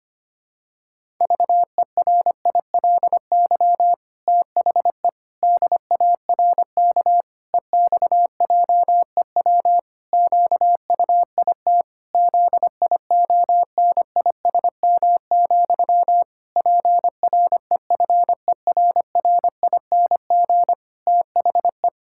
Morse code, heard as a keyed tone, about 25 wpm, 700 Hz. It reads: VERILY THE DARK EXJEW QUIT ZIONISM, PREFERRING THE
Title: Pangram Challenge v2 - 25wpm